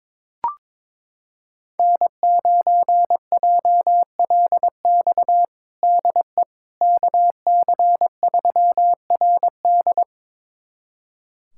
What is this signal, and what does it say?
Morse code, heard as a keyed tone, about 22 wpm, 700 Hz. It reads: N9JLX DE KC3RD
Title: USA Callsign DE Callsign 22wpm